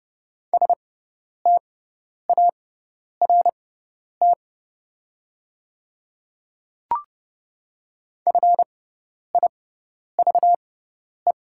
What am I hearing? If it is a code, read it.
Morse code, heard as a keyed tone, about 30 wpm, spaced as for 10 wpm, 700 Hz. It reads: START FIVE
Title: 6x Character Spacing - Top 500 Words - 30wpm